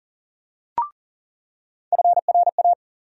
Morse code, heard as a keyed tone, about 40 wpm, 700 Hz. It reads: FRA
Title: USA Call Sign Postfixes - 3 Letters 40wpm